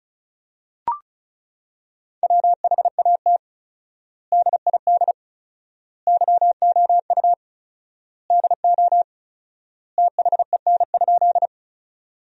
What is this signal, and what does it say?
Morse code, heard as a keyed tone, about 36 wpm, 700 Hz. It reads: WHAT DID YOU DO THEN?
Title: 4x Word-Spacing - Sentences from Top 100 Words - 35wpm